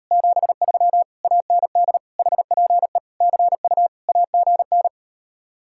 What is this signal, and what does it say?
Morse code, heard as a keyed tone, about 38 wpm, 700 Hz. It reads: 73 AND HPE CU AGN